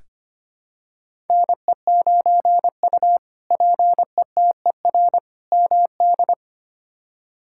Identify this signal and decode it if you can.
Morse code, heard as a keyed tone, about 25 wpm, 700 Hz. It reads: NE9U PETER MD
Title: K1USN SST QSO Practice - 25wpm